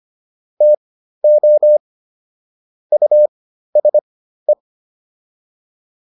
Morse code, heard as a keyed tone, about 25 wpm, spaced as for 13 wpm, 600 Hz. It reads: TO USE